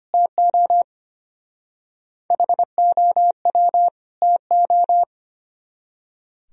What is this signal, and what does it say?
Morse code, heard as a keyed tone, about 31 wpm, 700 Hz. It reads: TO HOW TO
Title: Speed-Racing - Sets of 2 Words 25wpm